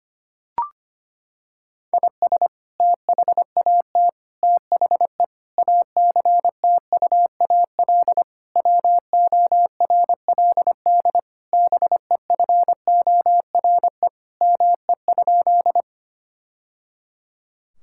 Morse code, heard as a keyed tone, about 25 wpm, 700 Hz. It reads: IS THAT THE ACTUAL WORLD BEFORE ME?
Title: Sentences from Top 1000 Words - Review Part 5 - 25wpm